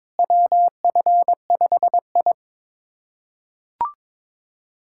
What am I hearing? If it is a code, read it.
Morse code, heard as a keyed tone, about 22 wpm, 700 Hz. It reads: WF5I
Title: USA Callsign DE Callsign - 4 Characters - 22wpm